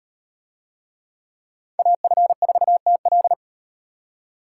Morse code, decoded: AF4TL

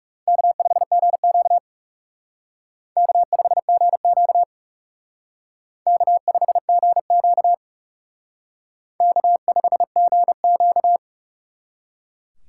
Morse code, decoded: K5GQ K5GQ K5GQ K5GQ

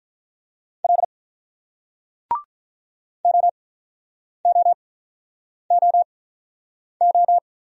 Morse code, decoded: L OOOO

